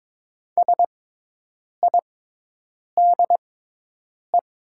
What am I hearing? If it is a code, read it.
Morse code, heard as a keyed tone, about 22 wpm, spaced as for 8 wpm, 700 Hz. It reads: SIDE
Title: 6x Character Spacing - Top 100 Words - 22wpm